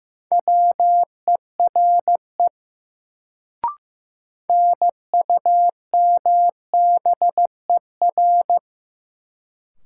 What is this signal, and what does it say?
Morse code, heard as a keyed tone, about 15 wpm, 700 Hz. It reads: WERE NUMBER
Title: Sentences from Top 100 Words - New Word Drill - 15wpm